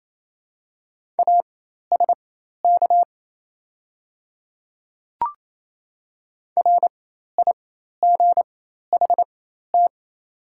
Morse code, heard as a keyed tone, about 28 wpm, spaced as for 13 wpm, 700 Hz. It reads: ASK RIGHT
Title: 4x Character Spacing - Top 300 Words - 28wpm